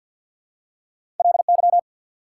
Morse code, heard as a keyed tone, about 48 wpm, 700 Hz. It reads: LY
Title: Prefixes and Suffixes 50wpm